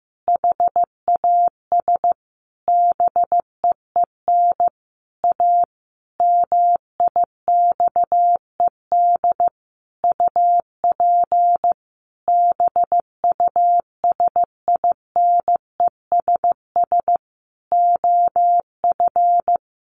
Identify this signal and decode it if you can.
Morse code, heard as a keyed tone, about 15 wpm, 700 Hz. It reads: HAS BEEN A MIXED UP BUSINESS OF
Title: Sentences from Top 1500 Words 15wpm